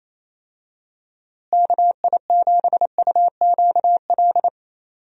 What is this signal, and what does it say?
Morse code, heard as a keyed tone, about 28 wpm, 700 Hz. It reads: KI7UQL
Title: USA Call Signs - 6 Characters - Repeated 2x - 28wpm